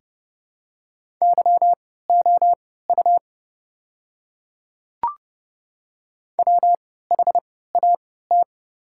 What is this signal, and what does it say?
Morse code, heard as a keyed tone, about 29 wpm, spaced as for 17 wpm, 700 Hz. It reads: YOU WHAT
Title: 3x Character Spacing - Top 100 Words - 30wpm